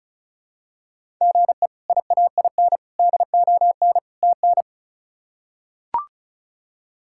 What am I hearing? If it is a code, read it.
Morse code, heard as a keyed tone, about 36 wpm, 700 Hz. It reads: GE IAIN DON TN
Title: K1USN SST QSO Practice - 35wpm